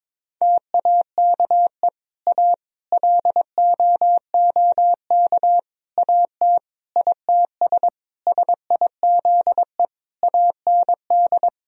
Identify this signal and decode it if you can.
Morse code, heard as a keyed tone, about 22 wpm, 700 Hz. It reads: TAKE A LOOK AT ITS SIZE AND
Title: Sentences from Top 1000 Words - Review Part 2 - 22wpm